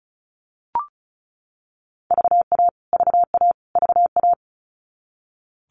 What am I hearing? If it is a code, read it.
Morse code, heard as a keyed tone, about 34 wpm, 700 Hz. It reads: VA VA VA
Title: US States Repeated 3x 35wpm